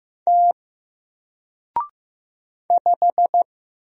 Morse code, heard as a keyed tone, about 15 wpm, 700 Hz. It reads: T 5
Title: Single Letter-Number 15wpm